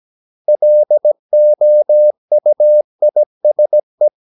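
Morse code, heard as a keyed tone, about 17 wpm, 600 Hz. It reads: LOUISE
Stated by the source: Slow (QRS) Morse practice at 17 wpm, from CW med Gnister.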